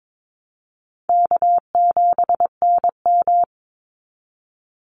Morse code, heard as a keyed tone, about 22 wpm, 700 Hz. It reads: K7NM
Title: Call Signs Repeated 2x - 22wpm